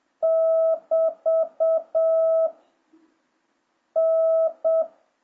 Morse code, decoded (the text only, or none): <BT> N